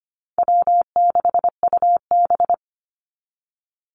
Morse code, heard as a keyed tone, about 25 wpm, 700 Hz. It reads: W6UB
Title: NA SOTA Chaser Callsigns 25wpm